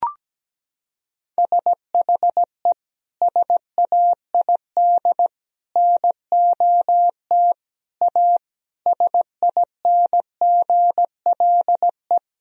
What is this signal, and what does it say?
Morse code, heard as a keyed tone, about 17 wpm, 700 Hz. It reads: SHE SAID NOT A SINGLE
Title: Sentences from Top 1000 Words - Review Part 4 - 17wpm